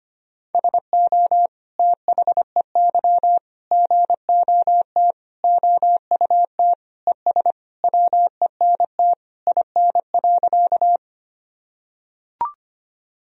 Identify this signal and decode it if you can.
Morse code, heard as a keyed tone, about 25 wpm, 700 Hz. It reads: SO THEY GOT OUT ES WENT IN.
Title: Sentences from Top 300 Words and Abbreviations 25wpm